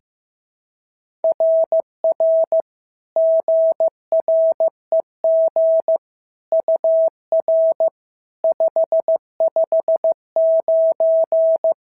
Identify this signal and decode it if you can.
Morse code, heard as a keyed tone, about 15 wpm, 650 Hz. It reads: RR GREG UR 559
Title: SOTA QSO Practice 15wpm